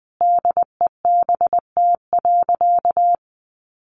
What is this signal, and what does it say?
Morse code, heard as a keyed tone, about 20 wpm, 700 Hz. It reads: DEBT.
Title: Sentences from Top 1300 Words 20wpm